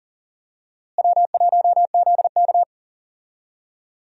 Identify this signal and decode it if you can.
Morse code, heard as a keyed tone, about 37 wpm, 700 Hz. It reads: W1ZK
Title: USA Call Signs - 4 Characters - Repeated 3x - 40wpm